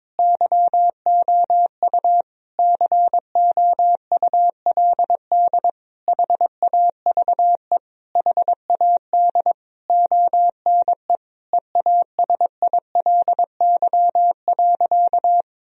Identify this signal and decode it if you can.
Morse code, heard as a keyed tone, about 22 wpm, 700 Hz. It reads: YOU COULD HAVE HAD ONE EASILY.
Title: Sentences from Top 1000 Words - Review Part 3 - 22wpm